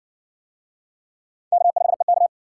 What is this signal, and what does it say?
Morse code, heard as a keyed tone, about 60 wpm, 700 Hz. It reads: K5EK